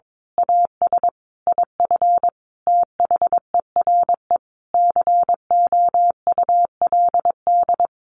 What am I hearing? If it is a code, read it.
Morse code, heard as a keyed tone, about 22 wpm, 700 Hz. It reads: AS IF THERE COULD
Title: A Sound Of Thunder 22wpm